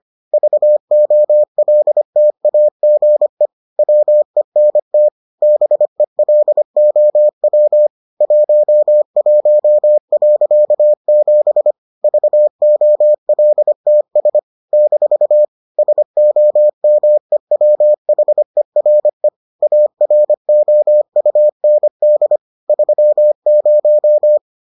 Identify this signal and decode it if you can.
Morse code, heard as a keyed tone, about 25 wpm, 600 Hz. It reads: VOLTAGE WENT BELOW 11.7 VOLTS - SOMEWHERE AROUND 30